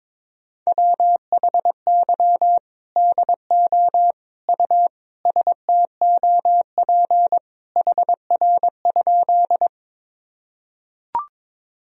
Morse code, heard as a keyed tone, about 22 wpm, 700 Hz. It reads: WHY DO U STOP HR?
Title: Sentences from Top 300 Words and Abbreviations 22wpm